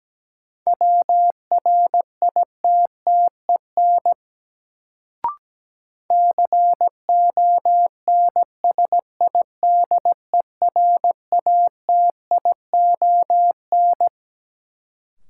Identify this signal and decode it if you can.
Morse code, heard as a keyed tone, about 17 wpm, 700 Hz. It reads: WRITTEN CONSIDERATION
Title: Sentences from Top 1500 Words - New Word Drill - 17wpm